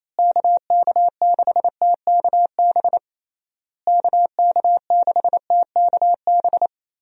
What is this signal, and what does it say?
Morse code, heard as a keyed tone, about 28 wpm, 700 Hz. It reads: KK6TKB KK6TKB
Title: USA Call Signs - 6 Characters - Repeated 3x - 28wpm